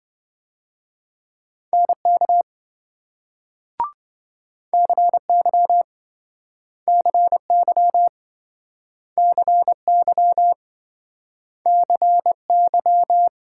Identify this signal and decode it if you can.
Morse code, heard as a keyed tone, about 26 wpm, 700 Hz. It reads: NK CY CY CY CY